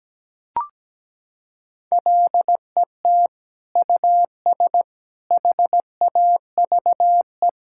Morse code, heard as a keyed tone, about 17 wpm, 700 Hz. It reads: LET US HAVE
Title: Sentences from Top 1000 Words - Review Part 6 - 17wpm